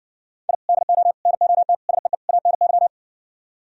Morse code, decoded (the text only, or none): I DO NOT SEE ANY